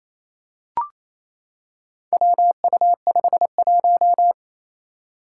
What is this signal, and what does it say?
Morse code, heard as a keyed tone, about 28 wpm, 700 Hz. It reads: WU51
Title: Four Challenging Letter-Number Combinations 28wpm